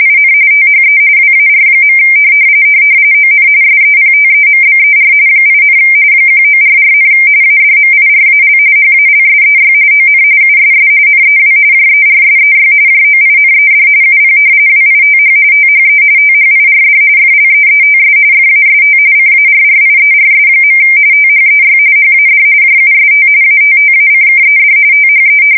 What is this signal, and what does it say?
RTTY, about 45.45 baud. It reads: IN THE WAR. ON SEPTEMBER 8, 1944, ADMIRAL HALSEY ON USS NEW JERSEY JOINED UP WITH THE THIRD FLEETS FAST CARRIERS. THEY HAD JUST ATTACKED THE
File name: fsk170 / battleshipnj